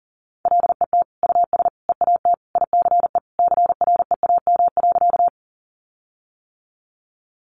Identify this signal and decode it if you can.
Morse code, heard as a keyed tone, about 40 wpm, 700 Hz. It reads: LET US EAT ICE CREAM.
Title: Sentences from Top 1500 Words 40wpm